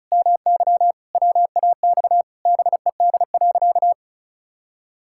Morse code, heard as a keyed tone, about 36 wpm, 700 Hz. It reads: MY WAX BED.
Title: Pangram Challenge v2 - 35wpm